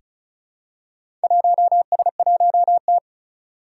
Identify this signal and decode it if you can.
Morse code, heard as a keyed tone, about 36 wpm, 700 Hz. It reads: 1S1T
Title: Four Challenging Letter-Number Combinations 35wpm